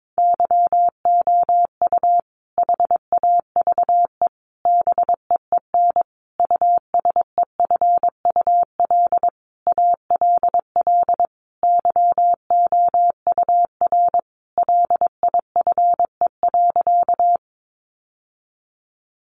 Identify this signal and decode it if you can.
Morse code, heard as a keyed tone, about 22 wpm, 700 Hz. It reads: YOU HAVE BEEN USEFUL ALL YOUR LIFE.
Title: Sentences from Top 700 Words 22wpm